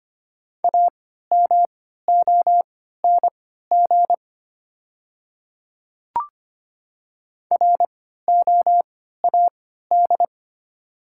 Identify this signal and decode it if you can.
Morse code, heard as a keyed tone, about 25 wpm, spaced as for 14 wpm, 690 Hz. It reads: AMONG ROAD
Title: 3x Character Spacing - Top 500 Words - 25wpm